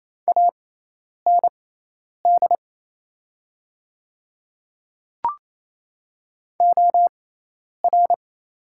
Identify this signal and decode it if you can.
Morse code, heard as a keyed tone, about 28 wpm, spaced as for 10 wpm, 700 Hz. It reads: AND OR
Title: 6x Character Spacing - Top 300 Words - 28wpm